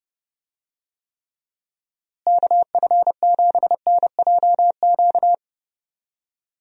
Morse code, heard as a keyed tone, about 30 wpm, 700 Hz. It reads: KF7NJQ
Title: USA Call Signs - 6 Characters - Repeated 3x - 30wpm